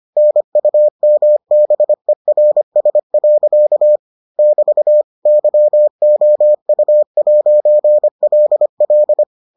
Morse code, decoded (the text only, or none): NUMBERS. <BT> YOU'LL